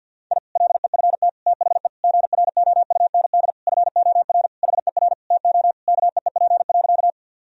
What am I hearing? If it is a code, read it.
Morse code, heard as a keyed tone, about 51 wpm, 700 Hz. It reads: I LEFT THE GROUND FOR HER TO KEEP.
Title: Sentences from Top 1000 Words - Review Part 2 - 50wpm